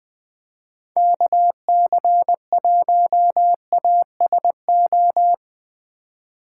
Morse code, heard as a keyed tone, about 20 wpm, 700 Hz. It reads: KC1ASO
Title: USA Call Signs - 6 Characters - 4x Character Spacing - 20wpm